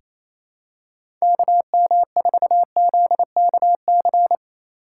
Morse code, heard as a keyed tone, about 28 wpm, 700 Hz. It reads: KM4ZKC
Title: USA Call Signs - 6 Characters - 2x Character Spacing - 28wpm